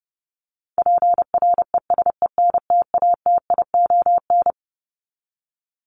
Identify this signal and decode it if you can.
Morse code, heard as a keyed tone, about 30 wpm, 700 Hz. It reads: PRESENTATION
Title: Top 1501-1750 Words 30wpm